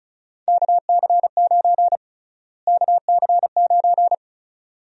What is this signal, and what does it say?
Morse code, heard as a keyed tone, about 36 wpm, 700 Hz. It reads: KC9 KC9
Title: USA Call Sign Prefixes - Repeated 2x - 35wpm